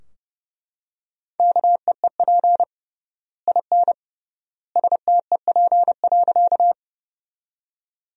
Morse code, heard as a keyed tone, about 30 wpm, 700 Hz. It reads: KEEP IN STEP.
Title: 3x Word-Spacing - Sentences from Top 400 Words - 30wpm